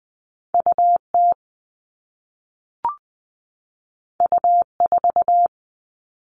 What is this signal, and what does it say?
Morse code, heard as a keyed tone, about 20 wpm, 700 Hz. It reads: UT U4
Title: All Cut Numbers - 2-Digit 20wpm